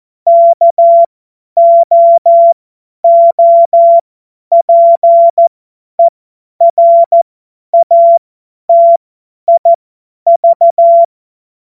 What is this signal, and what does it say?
Morse code, heard as a keyed tone, about 14 wpm, spaced as for 10 wpm, 680 Hz. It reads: KOOPERATIV
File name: Witz-05-August-2025-50-F70